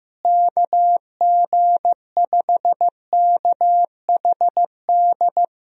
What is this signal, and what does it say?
Morse code, heard as a keyed tone, about 15 wpm, 700 Hz. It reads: KG5KHD
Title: USA Call Signs - 6 Characters - Repeated 3x - 15wpm